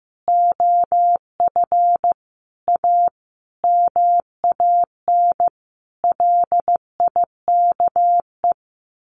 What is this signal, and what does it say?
Morse code, heard as a keyed tone, about 15 wpm, 700 Hz. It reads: OF A MAN LIKE
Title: Sentences from Top 1200 Words 15wpm